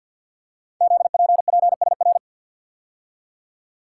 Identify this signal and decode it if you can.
Morse code, heard as a keyed tone, about 52 wpm, 700 Hz. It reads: ZPPSR
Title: Code Groups - 5 Letters - 50wpm